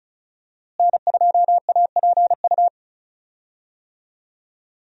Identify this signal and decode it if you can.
Morse code, heard as a keyed tone, about 36 wpm, 700 Hz. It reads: N2APU
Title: USA Call Signs - 5 Characters - Repeated 2x - 35wpm